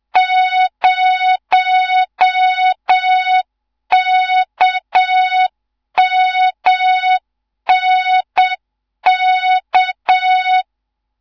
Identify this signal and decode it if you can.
Morse code, heard as a keyed tone, about 7 wpm, 3700 Hz. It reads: 0KMNK
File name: Morze-HA2MN / hang / 15 GYAKORLAT